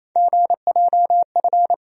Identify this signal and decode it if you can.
Morse code, heard as a keyed tone, about 28 wpm, 700 Hz. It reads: GJF
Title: USA Call Sign Postfixes - 3 Letters 28wpm